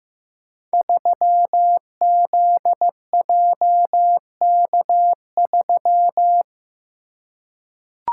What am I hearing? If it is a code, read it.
Morse code, heard as a keyed tone, about 15 wpm, 700 Hz. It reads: 3ZJK3
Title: Code Groups - 5 Letter:Numbers - 15wpm